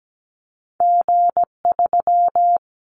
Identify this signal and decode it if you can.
Morse code, heard as a keyed tone, about 17 wpm, 700 Hz. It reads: G3
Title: Callsign Prefixes for Top 25 DXCC - 17wpm